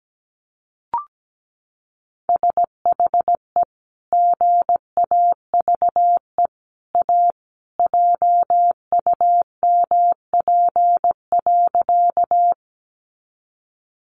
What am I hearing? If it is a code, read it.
Morse code, heard as a keyed tone, about 17 wpm, 700 Hz. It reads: SHE GAVE A JUMP.
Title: Sentences from Top 1300 Words 17wpm